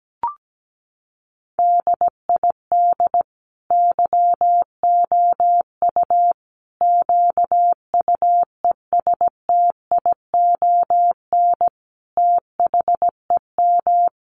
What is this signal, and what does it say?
Morse code, heard as a keyed tone, about 17 wpm, 700 Hz. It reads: DID YOU QUESTION THEM
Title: Sentences from Top 800 Words 17wpm